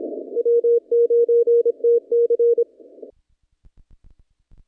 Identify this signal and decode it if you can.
Morse code, heard as a keyed tone, about 25 wpm, 470 Hz. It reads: W9TC
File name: contests / naqp-cw-k5tr-op-k5pi- -0300